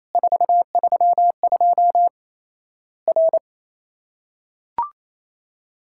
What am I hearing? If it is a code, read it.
Morse code, heard as a keyed tone, about 28 wpm, 700 Hz. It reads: 432 R
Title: Worked All Europe DX Contest - QTC Exchange - 28wpm